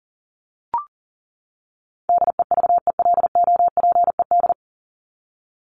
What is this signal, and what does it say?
Morse code, heard as a keyed tone, about 40 wpm, 700 Hz. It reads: DEVELOPED
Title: ED Suffix - Top 100 Words with Base Words 40wpm